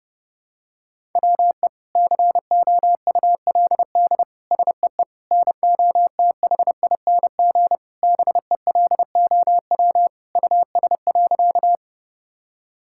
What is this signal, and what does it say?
Morse code, heard as a keyed tone, about 30 wpm, 700 Hz. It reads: WE COULD SEE NOTHING BELOW US.
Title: Sentences from Top 1000 Words - Review Part 5 - 30wpm